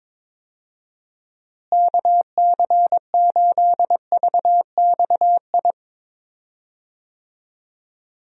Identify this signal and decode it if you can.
Morse code, heard as a keyed tone, about 22 wpm, 700 Hz. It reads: KC8VXI
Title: USA Call Signs - 6 Characters - Repeated 3x - 22wpm